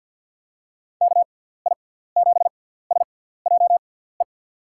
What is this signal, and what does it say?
Morse code, heard as a keyed tone, about 46 wpm, spaced as for 17 wpm, 700 Hz. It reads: KI7SJE
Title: USA Call Signs - 6 Characters - 6x Character Spacing - 50wpm